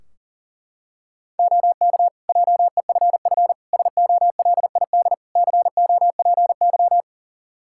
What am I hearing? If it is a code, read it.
Morse code, heard as a keyed tone, about 40 wpm, 700 Hz. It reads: OK JEFF SOLID COPY
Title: K1USN SST QSO Practice 40wpm